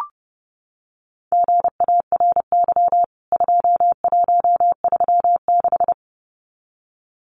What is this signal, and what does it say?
Morse code, heard as a keyed tone, about 30 wpm, 700 Hz. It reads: GARY 2136
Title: CWT Exchange 30wpm